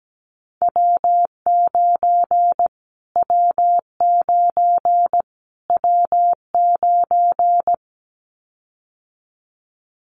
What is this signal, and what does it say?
Morse code, heard as a keyed tone, about 17 wpm, 700 Hz. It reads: W9 W9 W9
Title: Callsign Prefixes for Top 25 DXCC - Repeated 3x - 17wpm